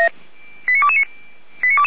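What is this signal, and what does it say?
Five-tone selective call (ZVEI), heard as quick runs of five tones, tones of about 70 ms. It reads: 89229
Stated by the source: ZVEI five-tone selective call 89230 received by SDR, from the Feuerwehr Orlamünde audio archive.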